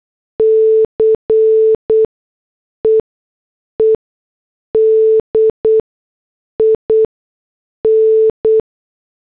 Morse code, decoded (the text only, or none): CEEDIN